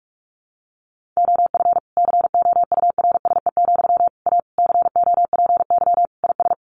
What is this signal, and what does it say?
Morse code, heard as a keyed tone, about 44 wpm, 700 Hz. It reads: OF COURSE, A COPY IS
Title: Sentences from Top 1400 Words 45wpm